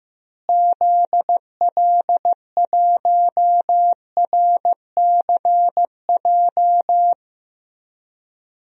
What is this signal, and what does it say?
Morse code, heard as a keyed tone, about 15 wpm, 700 Hz. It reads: ZL1RCJ
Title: DXCC-100 Call Signs - 6 Characters - Repeated 2x - 15wpm